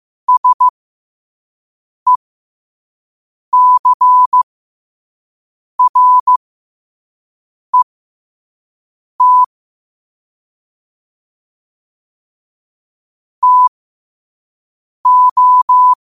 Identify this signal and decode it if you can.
Morse code, heard as a keyed tone, about 15 wpm, spaced as for 5 wpm, 1000 Hz. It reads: SECRET TO
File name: qotdfiles / 5wpm